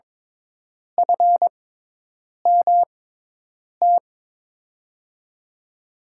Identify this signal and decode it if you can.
Morse code, heard as a keyed tone, about 21 wpm, spaced as for 8 wpm, 700 Hz. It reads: FMT
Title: USA Call Signs - 6x Character Spacing - 22wpm